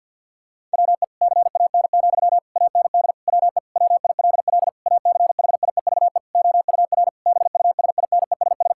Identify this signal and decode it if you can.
Morse code, heard as a keyed tone, about 50 wpm, 700 Hz. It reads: WE CAN, AND WE WILL ACHIEVE OUR BUSINESS
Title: Sentences from Top 1000 Words - Review Part 6 - 50wpm